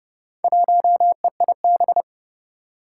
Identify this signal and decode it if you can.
Morse code, heard as a keyed tone, about 30 wpm, 700 Hz. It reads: 1EIB
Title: Four Challenging Letter-Number Combinations 30wpm